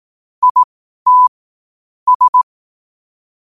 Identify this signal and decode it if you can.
Morse code, heard as a keyed tone, about 18 wpm, spaced as for 12 wpm, 1000 Hz. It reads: IT S
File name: qotdfiles / 12wpm